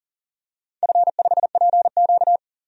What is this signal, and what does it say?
Morse code, heard as a keyed tone, about 40 wpm, 700 Hz. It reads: F5PQ
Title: DXCC-100 Call Signs - 4 Characters - 40wpm